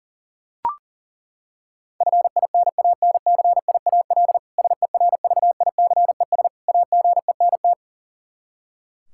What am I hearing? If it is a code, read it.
Morse code, heard as a keyed tone, about 40 wpm, 700 Hz. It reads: FINANCIAL SERVICES AGENT
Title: K1USN SST QSO Practice 40wpm